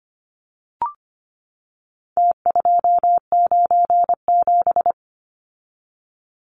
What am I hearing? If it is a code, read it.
Morse code, heard as a keyed tone, about 25 wpm, 700 Hz. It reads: T297
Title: Numbers - Contest Style - 4-Digit 25wpm